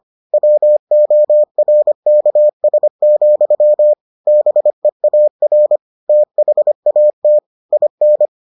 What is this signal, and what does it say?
Morse code, heard as a keyed tone, about 25 wpm, 600 Hz. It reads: WORKS, BEAR THAT IN